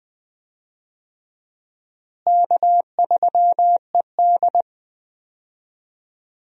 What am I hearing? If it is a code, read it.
Morse code, heard as a keyed tone, about 20 wpm, 700 Hz. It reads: K3ED
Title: USA Call Signs - 4 Characters - Repeated 3x - 20wpm